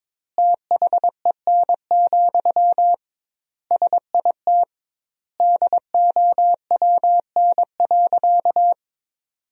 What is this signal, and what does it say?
Morse code, heard as a keyed tone, about 22 wpm, 700 Hz. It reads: THEN, SIT DOWN.
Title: 2x Word-Spacing - Sentences from Top 500 Words - 22wpm